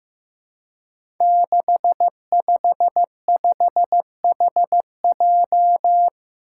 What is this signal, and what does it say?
Morse code, heard as a keyed tone, about 15 wpm, 700 Hz. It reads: 655HJ
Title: Five Challenging Letter-Number Combinations 15wpm